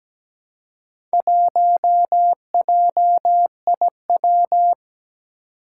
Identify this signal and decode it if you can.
Morse code, heard as a keyed tone, about 17 wpm, 700 Hz. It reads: 1JIW
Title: Four Challenging Letter-Number Combinations 17wpm